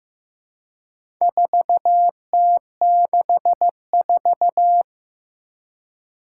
Morse code, heard as a keyed tone, about 15 wpm, 700 Hz. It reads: 4T64
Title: Four Challenging Letter-Number Combinations 15wpm